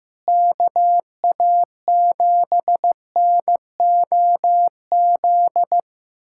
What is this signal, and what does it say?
Morse code, heard as a keyed tone, about 15 wpm, 700 Hz. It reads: KA7NOZ
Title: USA Callsign DE Callsign - 6 Characters - 15wpm